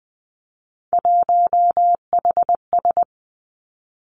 Morse code, heard as a keyed tone, about 20 wpm, 700 Hz. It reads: 1HS
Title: Three Challenging Letter-Number Combinations 20wpm